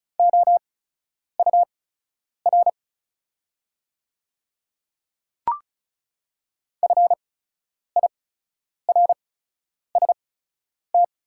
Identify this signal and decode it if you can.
Morse code, heard as a keyed tone, about 34 wpm, spaced as for 10 wpm, 700 Hz. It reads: OUR FIRST
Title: 8x Character Spacing - Top 300 Words - 35wpm